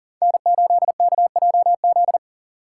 Morse code, heard as a keyed tone, about 40 wpm, 700 Hz. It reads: N8KJZ
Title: Code Groups - 5 Letter:Numbers - 40wpm